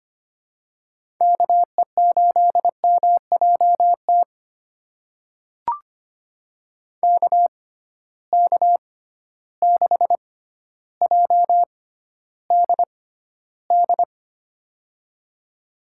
Morse code, heard as a keyed tone, about 25 wpm, spaced as for 13 wpm, 700 Hz. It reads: KE8MJT KK6JDD
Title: USA Call Signs - 6 Characters - 6x Character Spacing - 25wpm